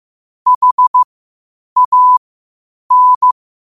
Morse code, heard as a keyed tone, about 16 wpm, spaced as for 8 wpm, 1000 Hz. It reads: HAN
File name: qotdfiles / 8wpm